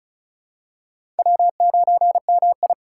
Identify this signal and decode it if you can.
Morse code, heard as a keyed tone, about 34 wpm, 700 Hz. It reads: W9MI
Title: USA Call Signs - 4 Characters - 3x Character Spacing - 35wpm